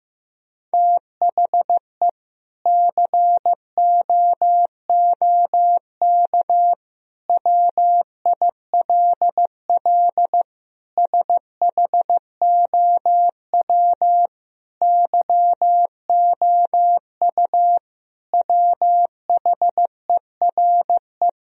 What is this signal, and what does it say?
Morse code, heard as a keyed tone, about 15 wpm, 700 Hz. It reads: THE COOK WILL SHOW YOU WHERE
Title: Sentences from Top 1000 Words - Review Part 4 - 15wpm